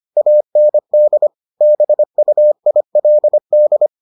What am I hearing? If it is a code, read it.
Morse code, heard as a keyed tone, about 25 wpm, 600 Hz. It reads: AND BUILD